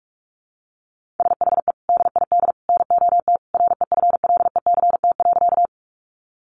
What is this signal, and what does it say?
Morse code, heard as a keyed tone, about 44 wpm, 700 Hz. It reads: SHE DID NOT REFLECT.
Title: Sentences from Top 1500 Words 45wpm